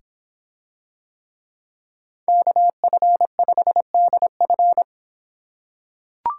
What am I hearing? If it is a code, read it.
Morse code, heard as a keyed tone, about 26 wpm, 700 Hz. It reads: KF5DF